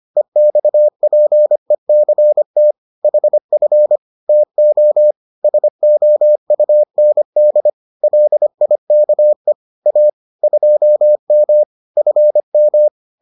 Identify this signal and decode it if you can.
Morse code, heard as a keyed tone, about 25 wpm, 600 Hz. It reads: EXPECT HF TO SOUND LIKE A 2M FM